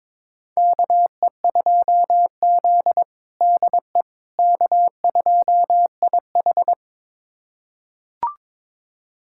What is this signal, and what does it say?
Morse code, heard as a keyed tone, about 22 wpm, 700 Hz. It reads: KE2Z DE K2IH
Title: USA Callsign DE Callsign - 4 Characters - 22wpm